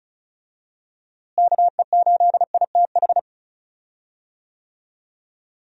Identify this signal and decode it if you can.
Morse code, heard as a keyed tone, about 34 wpm, 700 Hz. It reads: KE8ITH
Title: USA Call Signs - 6 Characters - Repeated 2x - 35wpm